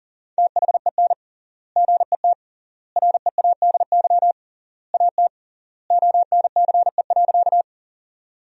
Morse code, heard as a keyed tone, about 40 wpm, 700 Hz. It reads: THEN GET READY AT ONCE.
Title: 3x Word-Spacing - Sentences from Top 400 Words - 40wpm